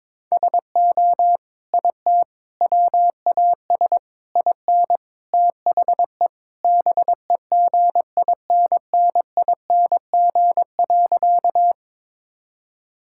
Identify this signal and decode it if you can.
Morse code, heard as a keyed tone, about 22 wpm, 700 Hz. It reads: SO IT WAS IN THE BEGINNING.
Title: Sentences from Top 1000 Words - Review Part 5 - 22wpm